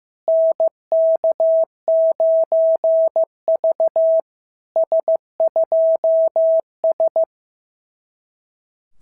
Morse code, heard as a keyed tone, about 15 wpm, 650 Hz. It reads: NK9V S2S
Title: SOTA QSO Practice 15wpm